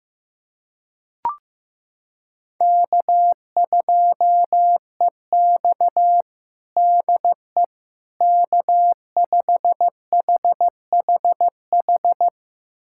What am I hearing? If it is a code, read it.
Morse code, heard as a keyed tone, about 15 wpm, 700 Hz. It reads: K2EX DE K5HHH
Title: USA Callsign DE Callsign 15wpm